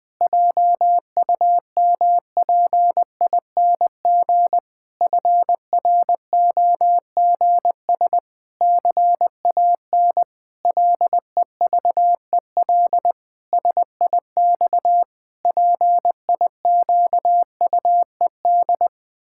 Morse code, heard as a keyed tone, about 21 wpm, 700 Hz. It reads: JUMPING FROGS CAN LEVEL SIX PIQUED
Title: Pangram Challenge v2 - 20wpm